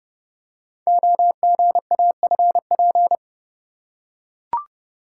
Morse code, heard as a keyed tone, about 28 wpm, 700 Hz. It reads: OGAFP
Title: Code Groups - 5 Letters - 30wpm